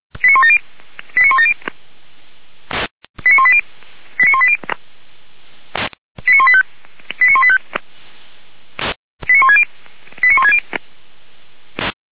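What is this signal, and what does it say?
Five-tone selective call (ZVEI), heard as quick runs of five tones, tones of about 70 ms. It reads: 89170 89170 89180 89180 89185 89185 89160 89160
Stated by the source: ZVEI five-tone call received by SDR.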